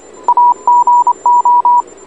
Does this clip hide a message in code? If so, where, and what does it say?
Yes: Morse code from 0.3 to 1.8 s, heard as a keyed tone, about 26 wpm, 960 Hz. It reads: AGO